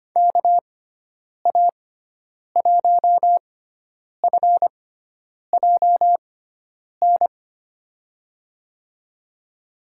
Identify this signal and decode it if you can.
Morse code, heard as a keyed tone, about 25 wpm, spaced as for 9 wpm, 700 Hz. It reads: KA1FJN
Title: USA Call Signs - 6 Characters - 6x Character Spacing - 25wpm